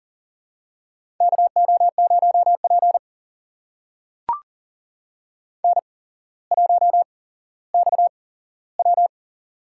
Morse code, heard as a keyed tone, about 42 wpm, spaced as for 17 wpm, 700 Hz. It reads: KO0P N1XW
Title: USA Call Signs - 4 Characters - 8x Character Spacing - 40wpm